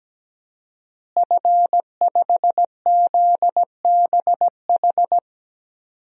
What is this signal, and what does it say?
Morse code, heard as a keyed tone, about 17 wpm, 700 Hz. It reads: F5ZBH
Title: DXCC-100 Call Signs - 5 Characters - Repeated 2x - 17wpm